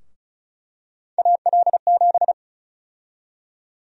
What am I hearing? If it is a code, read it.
Morse code, heard as a keyed tone, about 36 wpm, 700 Hz. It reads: AL7
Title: USA Call Sign Prefixes - 35wpm